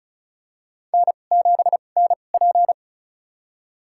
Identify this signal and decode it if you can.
Morse code, heard as a keyed tone, about 36 wpm, spaced as for 25 wpm, 700 Hz. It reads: N7NP
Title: USA Call Signs - 4 Characters - 2x Character Spacing - 35wpm